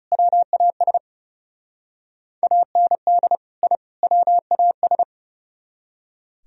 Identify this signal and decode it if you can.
Morse code, heard as a keyed tone, about 38 wpm, 700 Hz. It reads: WAS AND I WAS